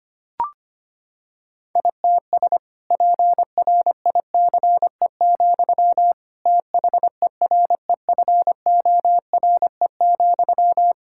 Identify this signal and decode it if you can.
Morse code, heard as a keyed tone, about 25 wpm, 700 Hz. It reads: ITS PRICE, THEREFORE,
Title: Sentences from Top 1000 Words - Review Part 5 - 25wpm